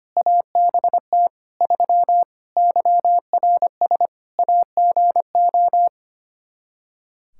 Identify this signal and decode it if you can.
Morse code, heard as a keyed tone, about 25 wpm, 700 Hz. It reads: ABT 3 YRS AGO
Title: Practice QSO - Exchange 3 - More Info - 25wpm